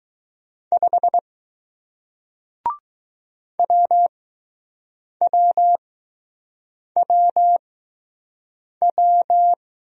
Morse code, heard as a keyed tone, about 20 wpm, spaced as for 6 wpm, 700 Hz. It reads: 5 WWWW